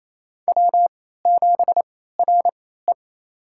Morse code, heard as a keyed tone, about 28 wpm, spaced as for 16 wpm, 700 Hz. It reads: W7RE
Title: USA Call Signs - 4 Characters - 3x Character Spacing - 28wpm